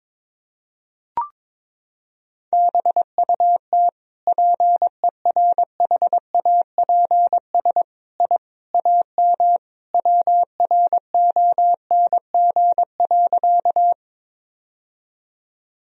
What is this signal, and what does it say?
Morse code, heard as a keyed tone, about 22 wpm, 700 Hz. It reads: BUT PERHAPS I AM WRONG.
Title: Sentences from Top 1000 Words - Review Part 2 - 22wpm